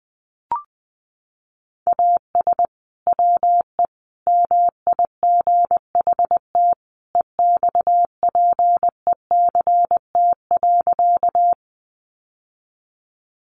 Morse code, heard as a keyed tone, about 20 wpm, 700 Hz. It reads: AS WE MIGHT EXPECT.
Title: Sentences from Top 1100 Words 20wpm